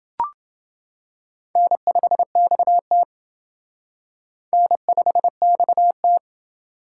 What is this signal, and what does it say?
Morse code, heard as a keyed tone, about 27 wpm, 700 Hz. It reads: N5XT N5XT